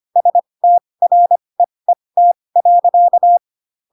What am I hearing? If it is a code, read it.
Morse code, heard as a keyed tone, about 25 wpm, spaced as for 20 wpm, 700 Hz. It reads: STREET.